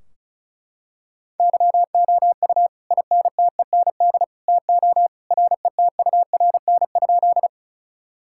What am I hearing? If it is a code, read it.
Morse code, heard as a keyed tone, about 36 wpm, 700 Hz. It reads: YOU INTEND TO RETURN?
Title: Sentences from Top 1600 Words 35wpm